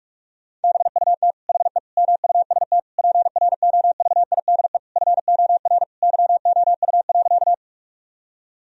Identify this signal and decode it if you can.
Morse code, heard as a keyed tone, about 46 wpm, 700 Hz. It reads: BUT HE MUST PROVIDE FOR YOU.
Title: Sentences from Top 1000 Words - Review Part 3 - 45wpm